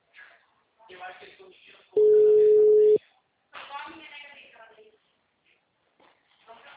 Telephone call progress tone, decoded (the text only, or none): ringback tone